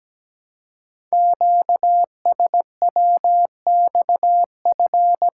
Morse code, heard as a keyed tone, about 17 wpm, 700 Hz. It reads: QSWXF
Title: Code Groups - 5 Letters - 17wpm